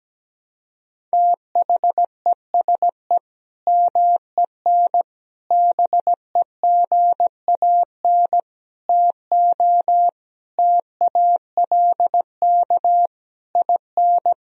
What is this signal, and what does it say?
Morse code, heard as a keyed tone, about 17 wpm, 700 Hz. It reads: THESE MEN BEGAN TO TALK IN
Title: Sentences from Top 1000 Words - Review Part 1 - 17wpm